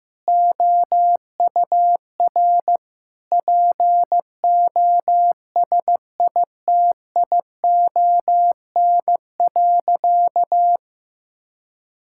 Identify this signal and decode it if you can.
Morse code, heard as a keyed tone, about 15 wpm, 700 Hz. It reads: OUR POSITION.
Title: Sentences from Top 1000 Words - Review Part 5 - 15wpm